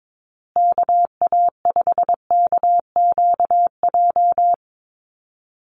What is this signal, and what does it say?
Morse code, heard as a keyed tone, about 22 wpm, 700 Hz. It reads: KA5KQJ
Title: USA Call Signs Repeated 2x 22wpm